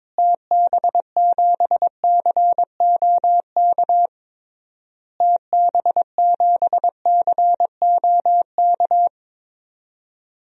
Morse code, heard as a keyed tone, about 22 wpm, 700 Hz. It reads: TB7COK TB7COK
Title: DXCC-100 Call Signs - 6 Characters - Repeated 3x - 22wpm